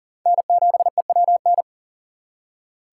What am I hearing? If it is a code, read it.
Morse code, heard as a keyed tone, about 50 wpm, 700 Hz. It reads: N7EWN
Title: Speed Racing - USA 1x3 Call Signs 40wpm